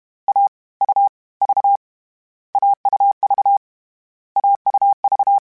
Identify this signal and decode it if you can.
Morse code, heard as a keyed tone, about 31 wpm, spaced as for 25 wpm, 800 Hz. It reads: AUV AUV AUV
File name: Difficult Combos / Difficult Combos-mp3s / AUV ShortRun @30